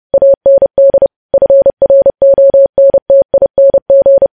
Morse code, heard as a keyed tone, about 30 wpm, 550 Hz. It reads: AND FRONTING
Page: news-headlines-in-morse-code-at-30-wpm